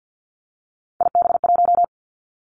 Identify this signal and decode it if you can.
Morse code, heard as a keyed tone, about 50 wpm, 700 Hz. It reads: IB1
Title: Three Challenging Letter-Number Combinations 50wpm